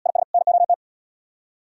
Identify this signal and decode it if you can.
Morse code, heard as a keyed tone, about 72 wpm, 700 Hz. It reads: IS NOT